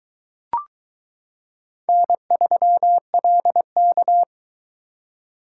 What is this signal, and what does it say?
Morse code, heard as a keyed tone, about 22 wpm, 700 Hz. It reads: N3LK